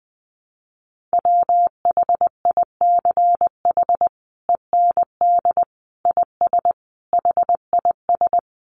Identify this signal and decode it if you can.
Morse code, heard as a keyed tone, about 20 wpm, 700 Hz. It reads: WHICH END IS HIS
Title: Sentences from Top 300 Words 20wpm